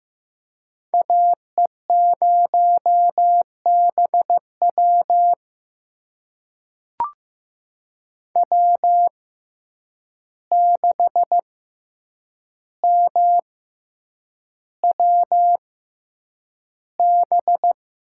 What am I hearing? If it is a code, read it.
Morse code, heard as a keyed tone, about 15 wpm, spaced as for 8 wpm, 700 Hz. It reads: AE0BW W6MWB
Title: USA Call Signs - 5 Characters - 6x Character Spacing - 15wpm